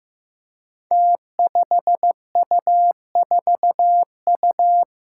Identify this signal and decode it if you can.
Morse code, heard as a keyed tone, about 15 wpm, 700 Hz. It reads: T5U4U
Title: Five Challenging Letter-Number Combinations 15wpm